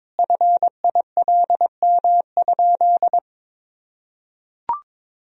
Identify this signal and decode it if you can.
Morse code, heard as a keyed tone, about 22 wpm, 700 Hz. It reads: FILM?
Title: Sentences from Top 1000 Words - Review Part 3 - 22wpm